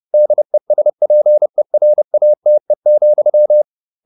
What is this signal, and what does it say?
Morse code, heard as a keyed tone, about 30 wpm, 600 Hz. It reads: DESPERATE,